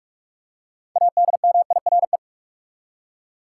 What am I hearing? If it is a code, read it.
Morse code, heard as a keyed tone, about 44 wpm, 700 Hz. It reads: ADMIRE
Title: Six Letter Words - 45wpm